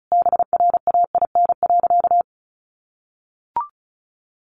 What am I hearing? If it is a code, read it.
Morse code, heard as a keyed tone, about 34 wpm, 700 Hz. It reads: BRAIN.
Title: Sentences from Top 1100 Words 35wpm